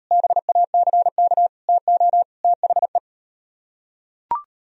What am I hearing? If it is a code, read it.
Morse code, heard as a keyed tone, about 38 wpm, 700 Hz. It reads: BACK TO THE